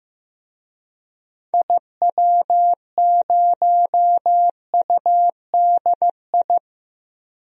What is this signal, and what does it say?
Morse code, heard as a keyed tone, about 16 wpm, 700 Hz. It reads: IW0UDI
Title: DXCC-100 Call Signs - 6 Characters - Repeated 2x - 15wpm